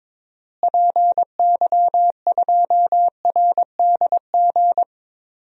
Morse code, heard as a keyed tone, about 21 wpm, 700 Hz. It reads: PY2RDG
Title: DXCC-100 Call Signs - 6 Characters - Repeated 3x - 22wpm